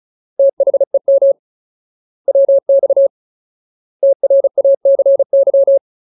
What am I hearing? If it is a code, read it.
Morse code, heard as a keyed tone, about 34 wpm, 550 Hz. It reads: THEM WX TRACY
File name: Ultimate Word List / 35WPM Ultimate